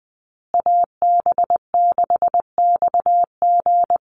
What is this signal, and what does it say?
Morse code, heard as a keyed tone, about 20 wpm, 700 Hz. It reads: AB6XG
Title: Call Signs Repeated 2x - 20wpm